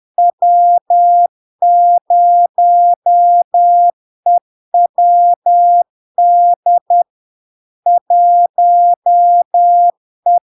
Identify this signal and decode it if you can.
Morse code, heard as a keyed tone, about 10 wpm, 700 Hz. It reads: W0EWD 1E
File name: LARK-FD-2022-Log-in-CW-10wpm-700-Hz